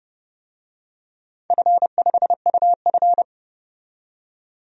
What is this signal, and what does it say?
Morse code, heard as a keyed tone, about 30 wpm, 700 Hz. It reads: F5UF
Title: DXCC-100 Call Signs - 4 Characters - 30wpm